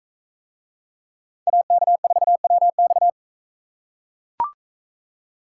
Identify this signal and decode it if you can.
Morse code, heard as a keyed tone, about 42 wpm, 700 Hz. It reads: AK4WX